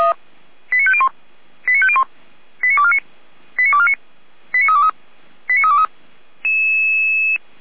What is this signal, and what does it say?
Five-tone selective call (ZVEI), heard as quick runs of five tones, tones of about 70 ms. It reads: 89501 89501 89249 89249 89323 89323